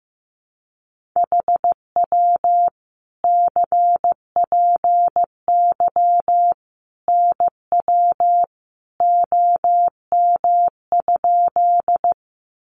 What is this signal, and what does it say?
Morse code, heard as a keyed tone, about 15 wpm, 700 Hz. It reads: HW CPY NW OM?
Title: Common QSO Phrases 15wpm